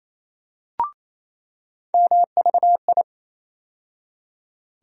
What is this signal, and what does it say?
Morse code, heard as a keyed tone, about 28 wpm, 700 Hz. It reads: MVI
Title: USA Call Sign Postfixes - 3 Letters 28wpm